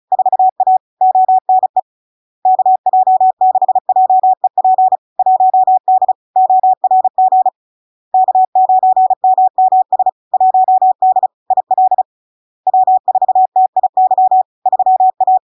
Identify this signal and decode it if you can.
Morse code, heard as a keyed tone, about 36 wpm, 750 Hz. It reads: 4A ONE KJ6JEP 1D ORG K9MMS 1D IL W4TIY 3A
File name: LARK-FD-2022-Log-in-CW-35wpm-750-Hz